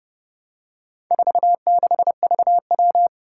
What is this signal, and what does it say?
Morse code, heard as a keyed tone, about 30 wpm, 700 Hz. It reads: 46VW
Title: Four Challenging Letter-Number Combinations 30wpm